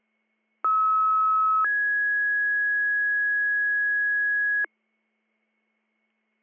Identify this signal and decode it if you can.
Two-tone page, heard as a one-second tone, then a three-second tone, about 1290 then 1650 Hz. It is heard from 0.6 to 4.7 s.